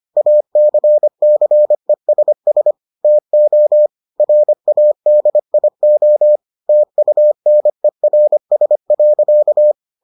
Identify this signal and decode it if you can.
Morse code, heard as a keyed tone, about 25 wpm, 600 Hz. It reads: ACCESS TO RADIO TUNERS.